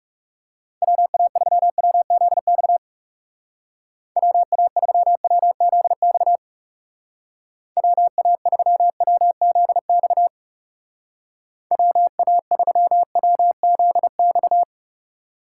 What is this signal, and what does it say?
Morse code, heard as a keyed tone, about 36 wpm, 700 Hz. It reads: WA3WZX WA3WZX WA3WZX WA3WZX
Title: Speed Racing - USA 2x3 Call Signs 30wpm